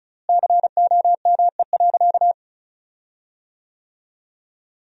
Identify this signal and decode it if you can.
Morse code, heard as a keyed tone, about 36 wpm, 700 Hz. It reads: COME.
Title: 4x Word-Spacing - Sentences from Top 100 Words - 35wpm